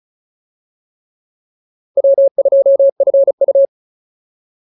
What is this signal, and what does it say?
Morse code, heard as a keyed tone, about 34 wpm, 550 Hz. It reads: W2FU